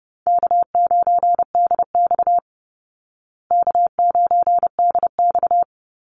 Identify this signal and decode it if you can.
Morse code, heard as a keyed tone, about 28 wpm, 700 Hz. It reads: K9DX K9DX
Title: Easy Call Signs Repeated 2x - 30wpm